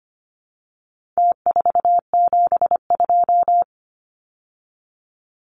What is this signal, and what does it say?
Morse code, heard as a keyed tone, about 25 wpm, 700 Hz. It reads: T472
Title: Numbers - Contest Style - 4-Digit 25wpm